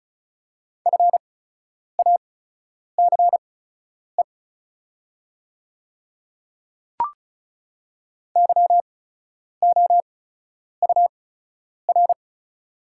Morse code, heard as a keyed tone, about 36 wpm, spaced as for 10 wpm, 700 Hz. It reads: FACE YOUR
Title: 8x Character Spacing - Top 300 Words - 35wpm